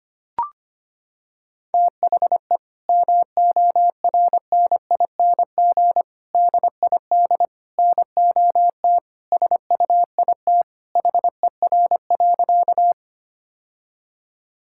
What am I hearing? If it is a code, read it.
Morse code, heard as a keyed tone, about 25 wpm, 700 Hz. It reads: THE MORNING DID NOT SUIT HER.
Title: Sentences from Top 1600 Words 25wpm